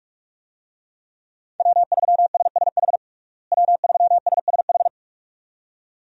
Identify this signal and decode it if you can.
Morse code, heard as a keyed tone, about 41 wpm, 700 Hz. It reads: W3SSH W3SSH
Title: USA Call Signs - 5 Characters - Repeated 2x - 45wpm